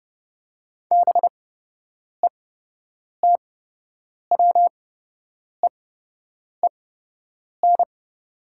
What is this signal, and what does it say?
Morse code, heard as a keyed tone, about 30 wpm, spaced as for 8 wpm, 700 Hz. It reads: BETWEEN
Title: 8x Character Spacing - Top 500 Words - 30wpm